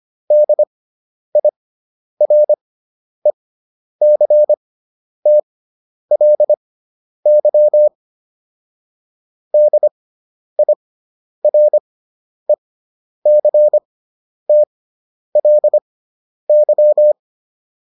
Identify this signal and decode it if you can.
Morse code, heard as a keyed tone, about 25 wpm, spaced as for 10 wpm, 600 Hz. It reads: DIRECTLY DIRECTLY